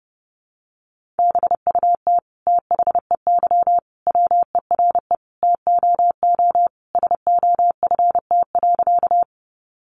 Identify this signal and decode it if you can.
Morse code, heard as a keyed tone, about 30 wpm, 700 Hz. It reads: BUT THEY WERE TOO SOFT.
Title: Sentences from Top 1500 Words 30wpm